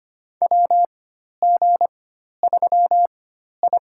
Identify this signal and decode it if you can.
Morse code, heard as a keyed tone, about 25 wpm, spaced as for 12 wpm, 700 Hz. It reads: WG3I
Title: USA Call Signs - 4x Character Spacing - 25wpm